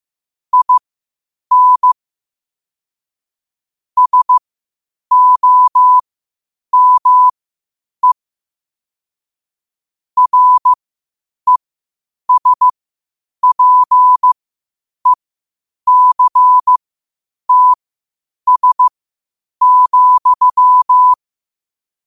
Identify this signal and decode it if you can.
Morse code, heard as a keyed tone, about 15 wpm, spaced as for 8 wpm, 1000 Hz. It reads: IN SOME RESPECTS,
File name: qotdfiles / 8wpm